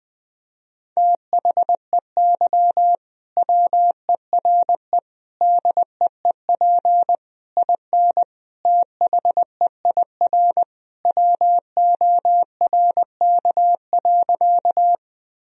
Morse code, heard as a keyed tone, about 20 wpm, 700 Hz. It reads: THEY WERE DEEP IN THEIR WORK.
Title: Sentences from Top 1000 Words - Review Part 3 - 20wpm